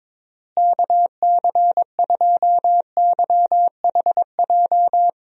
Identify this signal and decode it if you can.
Morse code, heard as a keyed tone, about 22 wpm, 700 Hz. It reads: KC2YHJ
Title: USA Call Signs - 6 Characters - Repeated 3x - 22wpm